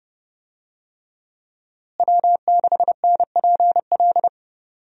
Morse code, heard as a keyed tone, about 30 wpm, 700 Hz. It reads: W6NPL
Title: USA Call Signs - 5 Characters - Repeated 3x - 30wpm